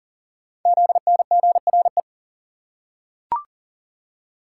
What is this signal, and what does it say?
Morse code, heard as a keyed tone, about 40 wpm, 700 Hz. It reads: ZNGRE
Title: Code Groups - 5 Letters - 40wpm